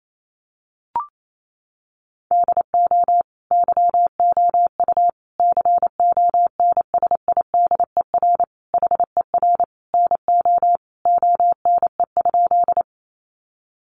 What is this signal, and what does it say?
Morse code, heard as a keyed tone, about 28 wpm, 700 Hz. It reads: DO YOU CONSIDER HER NO ONE?
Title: Sentences from Top 600 Words 28wpm